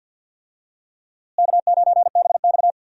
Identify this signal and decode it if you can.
Morse code, heard as a keyed tone, about 46 wpm, 700 Hz. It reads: K9BX
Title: USA Call Signs - 4 Characters - Repeated 2x - 50wpm